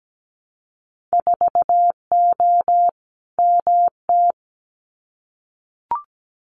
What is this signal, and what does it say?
Morse code, heard as a keyed tone, about 17 wpm, 700 Hz. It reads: 4O MT
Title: Winter Field Day Exchanges Encore 17wpm